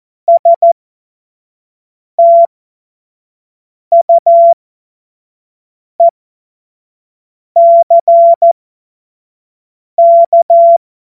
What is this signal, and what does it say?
Morse code, heard as a keyed tone, about 14 wpm, spaced as for 5 wpm, 680 Hz. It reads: STUECK
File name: Morsekurs / Witze / Witz 20. Januar 2026 25 F70